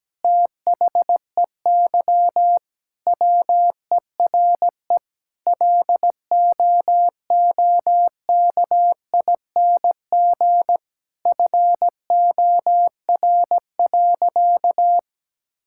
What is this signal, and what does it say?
Morse code, heard as a keyed tone, about 17 wpm, 700 Hz. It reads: THEY WERE LOOKING FOR.
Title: Sentences from Top 1000 Words - Review Part 6 - 17wpm